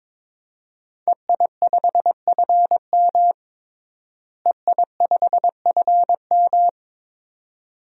Morse code, heard as a keyed tone, about 22 wpm, 700 Hz. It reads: EI5FM EI5FM
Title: DXCC-100 Call Signs - 5 Characters - Repeated 2x - 22wpm